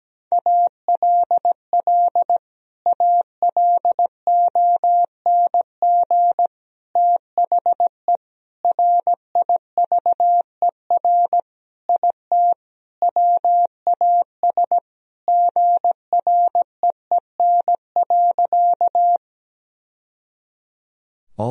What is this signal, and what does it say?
Morse code, heard as a keyed tone, about 17 wpm, 700 Hz. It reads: ALL ALONG THE RIVER IT WAS GREEN.
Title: Sentences from Top 1000 Words - Review Part 2 - 17wpm